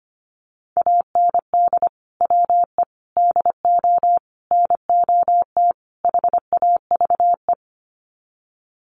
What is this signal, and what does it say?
Morse code, heard as a keyed tone, about 25 wpm, 700 Hz. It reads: AND WE DO NOT HAVE
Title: Sets of 5 Words Encore 25wpm